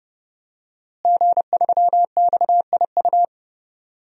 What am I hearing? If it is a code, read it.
Morse code, heard as a keyed tone, about 30 wpm, 700 Hz. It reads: G3XIU
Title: DXCC-100 Call Signs - 5 Characters - Repeated 2x - 30wpm